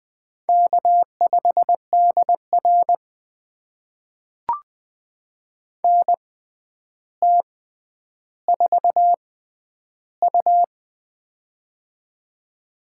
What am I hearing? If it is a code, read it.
Morse code, heard as a keyed tone, about 20 wpm, spaced as for 10 wpm, 700 Hz. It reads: K5DR NT4U
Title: USA Call Signs - 4 Characters - 6x Character Spacing - 20wpm